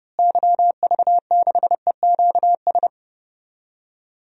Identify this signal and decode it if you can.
Morse code, heard as a keyed tone, about 31 wpm, 700 Hz. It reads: YV6EQS
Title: DXCC-100 Call Signs - 6 Characters - Repeated 3x - 30wpm